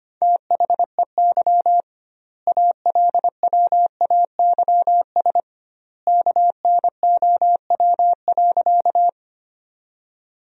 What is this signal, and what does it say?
Morse code, heard as a keyed tone, about 25 wpm, 700 Hz. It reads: THEY ALWAYS KNOW.
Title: 2x Word-Spacing - Sentences from Top 300 Words - 25wpm